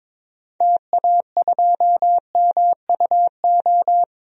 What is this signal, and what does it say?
Morse code, heard as a keyed tone, about 22 wpm, 700 Hz. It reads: TA2MUO
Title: DXCC-100 Call Signs - 6 Characters - Repeated 3x - 22wpm